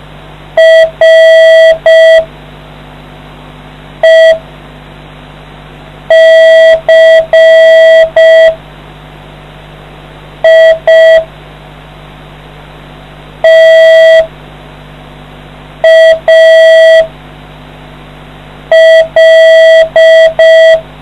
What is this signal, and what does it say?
Morse code, heard as a keyed tone, about 5 wpm, spaced as for 3 wpm, 1950 Hz. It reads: RECITAL